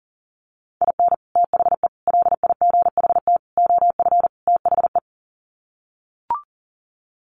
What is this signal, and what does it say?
Morse code, heard as a keyed tone, about 40 wpm, 700 Hz. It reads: IN THE LIGHT OF THE
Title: Sets of 5 Words Encore2 40wpm